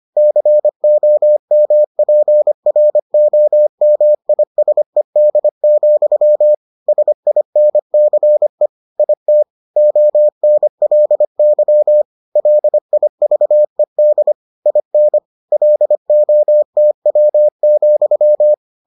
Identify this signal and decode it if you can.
Morse code, heard as a keyed tone, about 25 wpm, 600 Hz. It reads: COMPROMISED, SINCE IT ONLY LIVED IN LOTW,